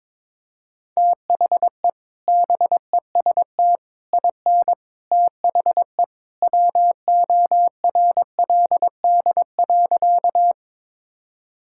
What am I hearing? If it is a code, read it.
Morse code, heard as a keyed tone, about 22 wpm, 700 Hz. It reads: THE BEST IN THE WORLD.
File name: practice / 22wpm / Sentences with 3-Word Phrases - 22wpm